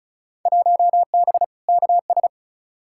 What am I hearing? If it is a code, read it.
Morse code, heard as a keyed tone, about 36 wpm, 700 Hz. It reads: 1B KS
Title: Field Day Exchanges Encore - 35wpm